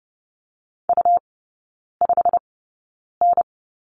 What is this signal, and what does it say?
Morse code, heard as a keyed tone, about 30 wpm, spaced as for 9 wpm, 700 Hz. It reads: U5N
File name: Lesson 17e - Letter u - Three Characters 30wpm